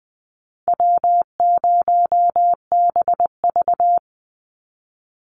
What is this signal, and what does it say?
Morse code, heard as a keyed tone, about 20 wpm, 700 Hz. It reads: W0BV
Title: Easy Call Signs - 20wpm